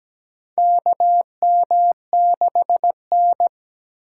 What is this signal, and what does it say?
Morse code, heard as a keyed tone, about 17 wpm, 700 Hz. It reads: KM6N
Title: USA Call Signs - 4 Characters - 2x Character Spacing - 17wpm